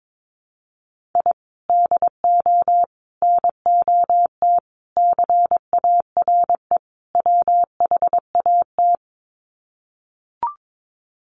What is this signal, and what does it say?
Morse code, heard as a keyed tone, about 22 wpm, 700 Hz. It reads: I DO NOT CARE WHAT
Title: Sets of 5 Words 22wpm